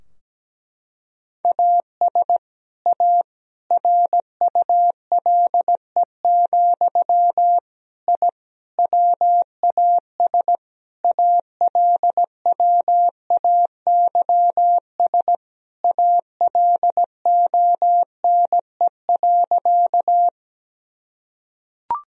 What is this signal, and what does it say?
Morse code, heard as a keyed tone, about 17 wpm, 700 Hz. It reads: AS A RULE, I WAS ALWAYS ALONE.
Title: Sentences from Top 1000 Words - Review Part 2 - 17wpm